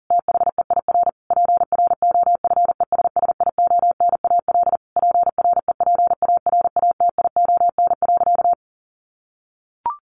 Morse code, heard as a keyed tone, about 40 wpm, 700 Hz. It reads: THEIR PROFESSIONAL PREPARATION.
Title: Sentences from Top 1400 Words 40wpm